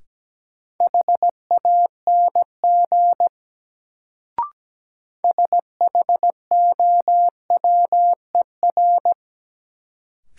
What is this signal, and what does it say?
Morse code, heard as a keyed tone, about 17 wpm, 700 Hz. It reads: HANG SHOWER
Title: Sentences from Top 1600 Words - New Word Drill - 17wpm